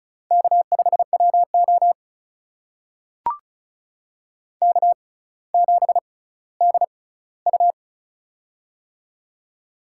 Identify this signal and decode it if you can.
Morse code, heard as a keyed tone, about 36 wpm, spaced as for 18 wpm, 700 Hz. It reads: K5WO K7DU
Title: USA Call Signs - 4 Characters - 6x Character Spacing - 35wpm